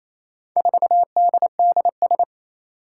Morse code, heard as a keyed tone, about 27 wpm, 700 Hz. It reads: 4DDS
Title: Four Challenging Letter-Number Combinations 28wpm